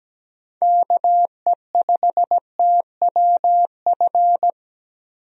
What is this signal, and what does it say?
Morse code, heard as a keyed tone, about 17 wpm, 700 Hz. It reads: KE5TWF
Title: USA Call Signs - 6 Characters - 17wpm